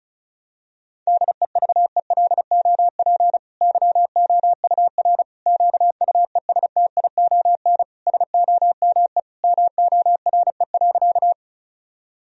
Morse code, heard as a keyed tone, about 36 wpm, 700 Hz. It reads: DEVELOP YOUR QUESTION SOME MORE.
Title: Sentences from Top 1000 Words - Review Part 2 - 35wpm